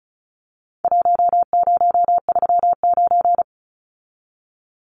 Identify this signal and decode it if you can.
Morse code, heard as a keyed tone, about 34 wpm, 700 Hz. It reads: 1039
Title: Numbers - Contest Style - 4-Digit 35wpm